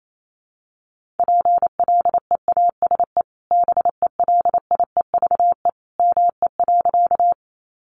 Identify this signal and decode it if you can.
Morse code, heard as a keyed tone, about 28 wpm, 700 Hz. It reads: PLEASE BELIEVE ME.
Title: Sentences from Top 1500 Words 28wpm